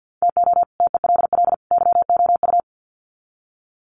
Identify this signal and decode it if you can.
Morse code, heard as a keyed tone, about 48 wpm, 700 Hz. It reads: TO TELL YOU
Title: Sets of 3 Words 50wpm